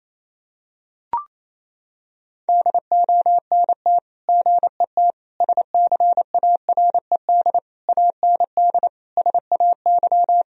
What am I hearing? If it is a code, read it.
Morse code, heard as a keyed tone, about 28 wpm, 700 Hz. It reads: DONT GET SCARED AND SAY
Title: Sentences from Top 1000 Words - Review Part 4 - 28wpm